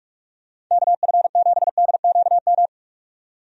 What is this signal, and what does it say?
Morse code, heard as a keyed tone, about 44 wpm, 700 Hz. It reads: KF7DQM
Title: USA Call Signs - 6 Characters - 45wpm